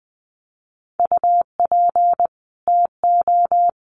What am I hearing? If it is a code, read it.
Morse code, heard as a keyed tone, about 20 wpm, 700 Hz. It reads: UP TO
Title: Sets of 2 Words 20wpm